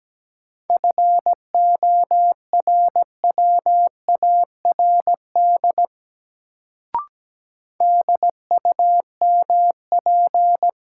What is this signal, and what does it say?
Morse code, heard as a keyed tone, about 17 wpm, 700 Hz. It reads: FORWARD DUMP
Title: Sentences from Top 1300 Words - New Word Drill - 17wpm